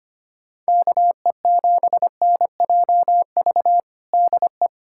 Morse code, heard as a keyed tone, about 25 wpm, 700 Hz. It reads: KE7NJV DE
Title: USA Callsign DE Callsign 25wpm